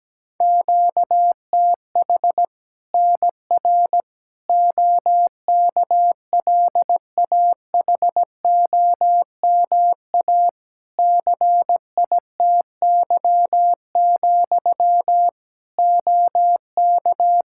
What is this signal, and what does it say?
Morse code, heard as a keyed tone, about 17 wpm, 700 Hz. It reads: QTH NR OKLAHOMA CITY, OK
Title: Practice QSO - Exchange 1 - Beginning - 17wpm